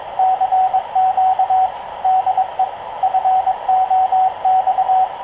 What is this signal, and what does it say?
Morse code, heard as a keyed tone, about 23 wpm, 750 Hz, with radio noise. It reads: CQ DE FOX
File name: k0frp ki0ii wa8bxn fox utc